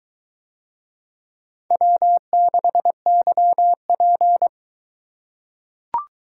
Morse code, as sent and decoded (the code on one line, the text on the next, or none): .-- -.... -.-- .--.
W6YP